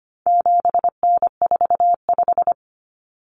Morse code, heard as a keyed tone, about 25 wpm, 700 Hz. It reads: 7N45
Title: Common Cut Numbers - 4-Digit 25wpm